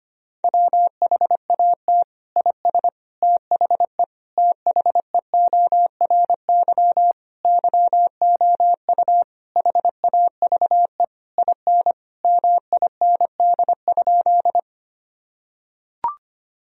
Morse code, heard as a keyed tone, about 25 wpm, 700 Hz. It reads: WHAT IS THE THEORY YOU HAVE IN MIND?
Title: Sentences from Top 1000 Words - Review Part 3 - 25wpm